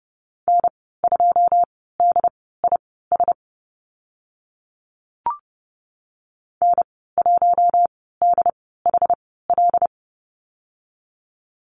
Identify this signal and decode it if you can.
Morse code, heard as a keyed tone, about 30 wpm, spaced as for 17 wpm, 700 Hz. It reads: N2DIS N1DHL
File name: Lesson 15g - Number 5 - Call Signs 30wpm